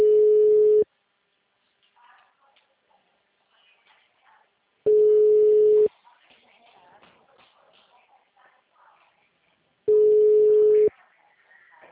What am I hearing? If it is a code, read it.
Ringback tone (Europe, 425 Hz), with speech.